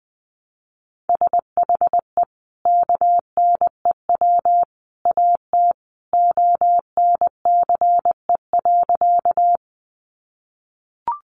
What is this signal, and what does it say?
Morse code, heard as a keyed tone, about 20 wpm, 700 Hz. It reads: SHE KNEW AT ONCE.
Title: Sentences from Top 400 Words 20wpm